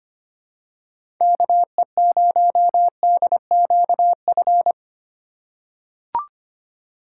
Morse code, heard as a keyed tone, about 25 wpm, 700 Hz. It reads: KE0DQF
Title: USA Call Signs - 6 Characters - 3x Character Spacing - 25wpm